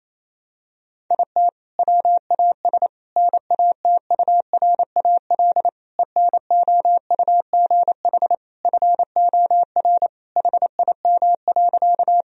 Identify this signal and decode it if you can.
Morse code, heard as a keyed tone, about 28 wpm, 700 Hz. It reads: IT WAS NATURAL ENOUGH FOR HIM.
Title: Sentences from Top 1000 Words - Review Part 4 - 28wpm